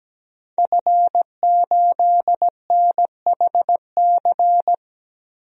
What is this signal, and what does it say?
Morse code, heard as a keyed tone, about 17 wpm, 700 Hz. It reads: F8NHC
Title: DXCC-100 Call Signs - 5 Characters - Repeated 3x - 17wpm